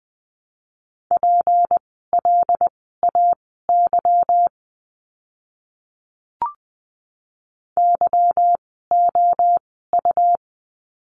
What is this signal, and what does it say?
Morse code, heard as a keyed tone, about 20 wpm, spaced as for 15 wpm, 700 Hz. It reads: PLAY YOU
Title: 153 Lesson 25f - Letter p - Words 20wpm